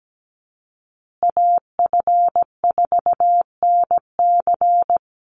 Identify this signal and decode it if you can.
Morse code, heard as a keyed tone, about 17 wpm, 700 Hz. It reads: AF4NC
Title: Call Signs Repeated 3x - 17wpm